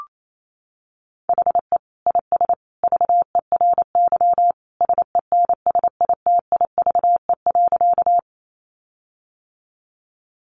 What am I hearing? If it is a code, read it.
Morse code, heard as a keyed tone, about 28 wpm, 700 Hz. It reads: HE IS VERY SENSITIVE.
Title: Sentences from Top 1400 Words 28wpm